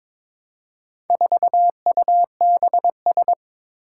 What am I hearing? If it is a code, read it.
Morse code, heard as a keyed tone, about 22 wpm, 700 Hz. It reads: 4UBS
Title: Four Challenging Letter-Number Combinations 22wpm